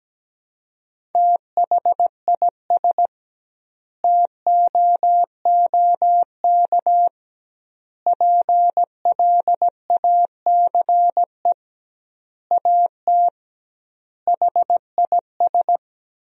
Morse code, heard as a keyed tone, about 17 wpm, 700 Hz. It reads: THIS TOOK PLACE AT HIS
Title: 2x Word-Spacing - Sentences from Top 300 Words - 17wpm